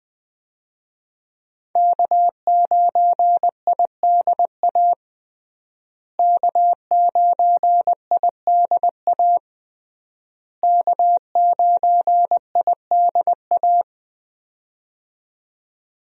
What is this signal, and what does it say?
Morse code, heard as a keyed tone, about 20 wpm, 700 Hz. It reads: K9IDA K9IDA K9IDA
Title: USA Call Signs - 5 Characters - Repeated 3x - 20wpm